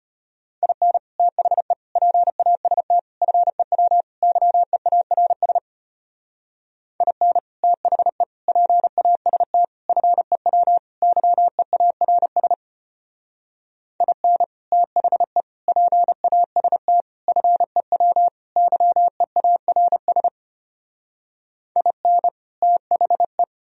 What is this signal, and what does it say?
Morse code, heard as a keyed tone, about 32 wpm, 700 Hz. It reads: IN THE PAST FEW YEARS IN THE PAST FEW YEARS IN THE PAST FEW YEARS IN THE